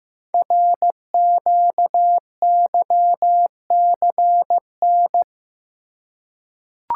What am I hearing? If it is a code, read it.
Morse code, heard as a keyed tone, about 15 wpm, 700 Hz. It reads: RQYCN
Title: Code Groups - 5 Letters - 15wpm